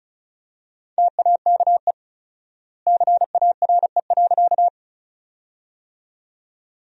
Morse code, heard as a keyed tone, about 36 wpm, 700 Hz. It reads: TAKE CARE.
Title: 4x Word-Spacing - Sentences from Top 300 Words - 35wpm